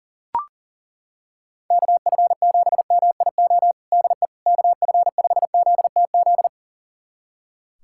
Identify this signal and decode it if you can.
Morse code, heard as a keyed tone, about 40 wpm, 700 Hz. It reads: KF7MIO DE KF5ZTZ
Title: USA Callsign DE Callsign 40wpm